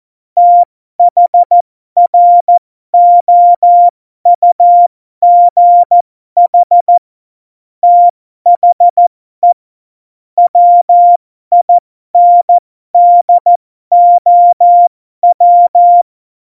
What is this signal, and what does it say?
Morse code, heard as a keyed tone, about 14 wpm, spaced as for 12 wpm, 700 Hz. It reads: THROUGH THE WINDOW
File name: Morsekurs / Witze